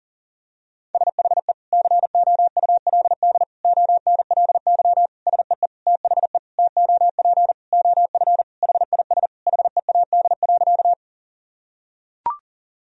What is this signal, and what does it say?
Morse code, heard as a keyed tone, about 40 wpm, 700 Hz. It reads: SHE COULD ONLY SEE THE TOP OF HIS HEAD.
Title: Sentences from Top 1000 Words - Review Part 2 - 40wpm